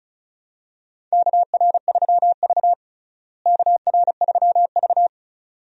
Morse code, heard as a keyed tone, about 34 wpm, 700 Hz. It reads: KR3V KR3V
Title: USA Call Signs - 4 Characters - Repeated 2x - 35wpm